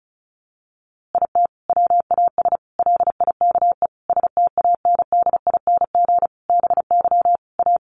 Morse code, heard as a keyed tone, about 34 wpm, 700 Hz. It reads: IT WAS LIKE STANDING BY A
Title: A Sound Of Thunder 35wpm